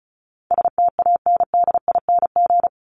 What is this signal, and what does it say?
Morse code, heard as a keyed tone, about 34 wpm, 700 Hz. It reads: STANDING
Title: ING Suffix - Top 100 Words with Base Words 35wpm